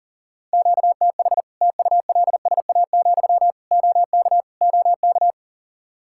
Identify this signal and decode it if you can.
Morse code, heard as a keyed tone, about 40 wpm, 700 Hz. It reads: QTH TULSA, OK OK
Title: Practice QSO - Exchange 1 - Beginning - 40wpm